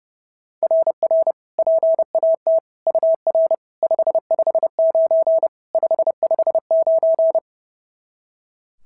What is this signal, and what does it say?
Morse code, heard as a keyed tone, about 30 wpm, 690 Hz. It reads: RR PAT UR 559 559
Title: SOTA QSO Practice 30wpm